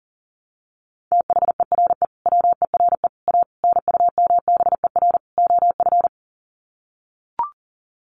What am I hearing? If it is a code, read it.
Morse code, heard as a keyed tone, about 40 wpm, 700 Hz. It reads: THERE WERE A NUMBER OF
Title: Sets of 5 Words Encore2 40wpm